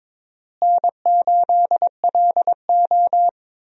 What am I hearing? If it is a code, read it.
Morse code, heard as a keyed tone, about 21 wpm, 700 Hz. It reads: N8LO
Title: USA Call Signs - 4 Characters - Repeated 2x - 22wpm